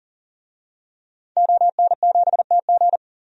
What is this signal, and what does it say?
Morse code, heard as a keyed tone, about 40 wpm, 700 Hz. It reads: ON7TG
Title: DXCC-100 Call Signs - 5 Characters - 40wpm